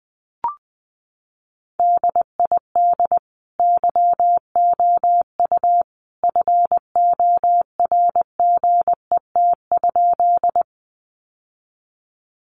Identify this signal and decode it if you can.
Morse code, heard as a keyed tone, about 19 wpm, 700 Hz. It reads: DID YOU FORGET?
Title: Sentences from Top 1200 Words 20wpm